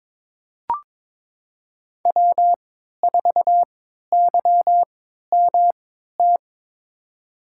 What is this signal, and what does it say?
Morse code, heard as a keyed tone, about 22 wpm, spaced as for 12 wpm, 700 Hz. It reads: W4YMT
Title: USA Call Signs - 5 Characters - 3x Character Spacing - 22wpm